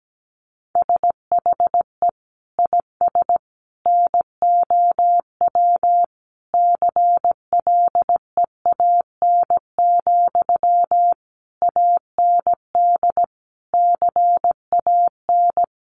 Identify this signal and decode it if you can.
Morse code, heard as a keyed tone, about 17 wpm, 700 Hz. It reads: SHE IS NOW CLEAN, AND CAN
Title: Sentences from Top 1100 Words 17wpm